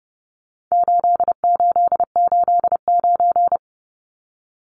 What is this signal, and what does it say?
Morse code, heard as a keyed tone, about 30 wpm, 700 Hz. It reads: 8889
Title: Numbers - 4-Digit 30wpm